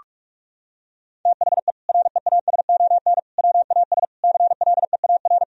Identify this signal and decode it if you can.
Morse code, heard as a keyed tone, about 41 wpm, 700 Hz. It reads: THE REASON WAS CLEAR
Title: Sentences from Top 1000 Words - Review Part 2 - 45wpm